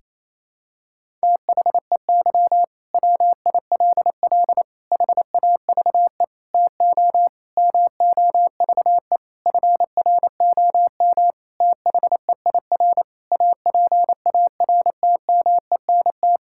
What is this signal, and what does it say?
Morse code, heard as a keyed tone, about 28 wpm, 700 Hz. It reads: THEY WILL HAVE TO MOVE FROM THEIR APARTMENT
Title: Sentences from Top 1000 Words - Review Part 6 - 28wpm